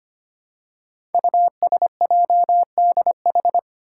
Morse code, heard as a keyed tone, about 25 wpm, 700 Hz. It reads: USJDH
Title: Code Groups - 5 Letters - 25wpm